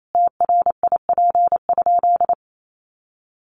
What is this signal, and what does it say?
Morse code, heard as a keyed tone, about 28 wpm, 700 Hz. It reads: TRIP?
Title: Sentences from Top 1300 Words 28wpm